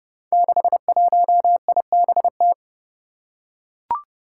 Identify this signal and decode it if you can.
Morse code, heard as a keyed tone, about 30 wpm, 700 Hz. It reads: 61IBT
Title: Five Challenging Letter-Number Combinations 30wpm